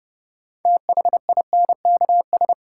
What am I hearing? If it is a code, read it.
Morse code, heard as a keyed tone, about 31 wpm, 700 Hz. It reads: THINKS
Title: Sentences from Top 1000 Words - Review Part 6 - New Word Drill - 30wpm